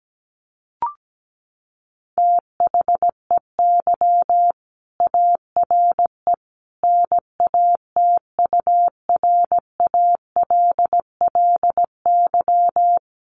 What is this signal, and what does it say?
Morse code, heard as a keyed tone, about 17 wpm, 700 Hz. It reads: THEY ARE NATURALLY
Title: Sentences from Top 1300 Words 17wpm